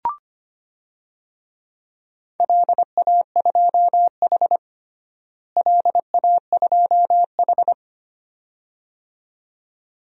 Morse code, heard as a keyed tone, about 25 wpm, 700 Hz. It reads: LA2H LA2H
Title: DXCC-100 Call Signs - 4 Characters - Repeated 2x - 25wpm